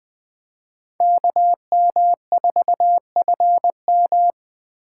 Morse code, heard as a keyed tone, about 21 wpm, 700 Hz. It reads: KM4FM
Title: Speed Racing - USA Call Signs 17wpm